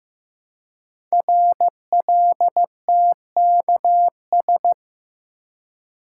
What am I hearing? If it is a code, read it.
Morse code, heard as a keyed tone, about 15 wpm, 700 Hz. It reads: RLTKS
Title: Code Groups - 5 Letters - 15wpm